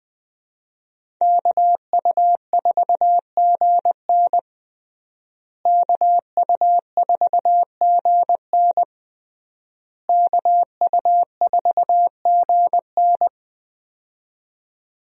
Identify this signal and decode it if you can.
Morse code, heard as a keyed tone, about 20 wpm, 700 Hz. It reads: KU4GN KU4GN KU4GN
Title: USA Call Signs - 5 Characters - Repeated 3x - 20wpm